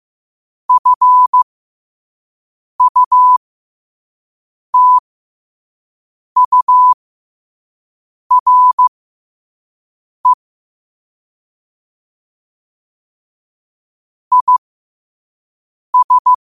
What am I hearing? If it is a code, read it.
Morse code, heard as a keyed tone, about 15 wpm, spaced as for 5 wpm, 1000 Hz. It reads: FUTURE IS
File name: qotdfiles / 5wpm